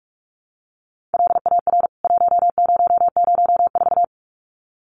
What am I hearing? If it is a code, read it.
Morse code, heard as a keyed tone, about 44 wpm, 700 Hz. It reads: LAR 1004
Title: CWT Exchange 45wpm